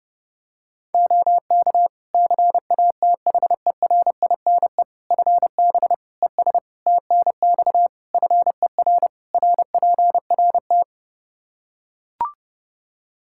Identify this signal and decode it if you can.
Morse code, heard as a keyed tone, about 32 wpm, 700 Hz. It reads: OK CATHERINE FB ES TNX FER RPRT
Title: Practice QSO - Exchange 2 - Info - 30wpm